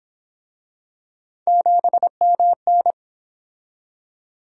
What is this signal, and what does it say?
Morse code, heard as a keyed tone, about 30 wpm, 700 Hz. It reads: 7MN